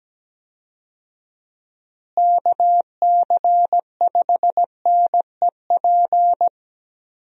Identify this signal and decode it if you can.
Morse code, heard as a keyed tone, about 17 wpm, 700 Hz. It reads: KC5NEP
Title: USA Call Signs - 6 Characters - Repeated 2x - 17wpm